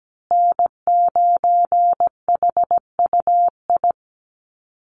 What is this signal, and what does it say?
Morse code, heard as a keyed tone, about 17 wpm, 700 Hz. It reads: N9HUI
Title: USA Call Signs Repeated 3x 17wpm